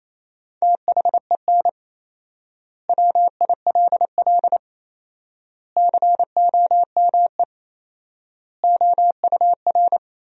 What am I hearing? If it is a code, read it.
Morse code, heard as a keyed tone, about 28 wpm, 700 Hz. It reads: THEN WILL COME OUR
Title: 4x Word-Spacing - Sentences from Top 200 Words - 28wpm